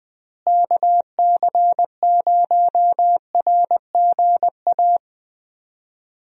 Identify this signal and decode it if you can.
Morse code, heard as a keyed tone, about 20 wpm, 700 Hz. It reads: KC0RGA
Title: USA Call Signs - 6 Characters - Repeated 2x - 20wpm